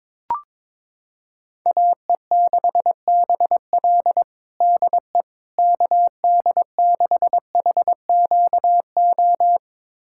Morse code, heard as a keyed tone, about 22 wpm, 700 Hz. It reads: AE6BL DE KD6HQO
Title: USA Callsign DE Callsign 22wpm